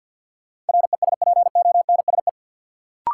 Morse code, heard as a keyed tone, about 48 wpm, 700 Hz. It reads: RESPONSE
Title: Sentences from Top 1400 Words - New Word Drill Alt - 50wpm